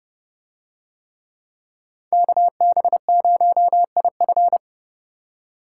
Morse code, heard as a keyed tone, about 30 wpm, 700 Hz. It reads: KB0IF